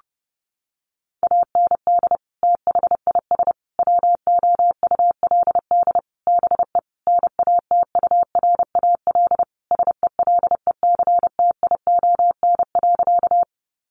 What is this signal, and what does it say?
Morse code, heard as a keyed tone, about 30 wpm, 700 Hz. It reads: AND THIS WOULD BE NATURAL SELECTION.
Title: Sentences from Top 1400 Words 30wpm